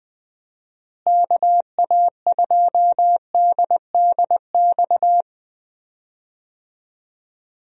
Morse code, heard as a keyed tone, about 20 wpm, 700 Hz. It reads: KA2DDX
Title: USA Call Signs - 6 Characters - Repeated 2x - 20wpm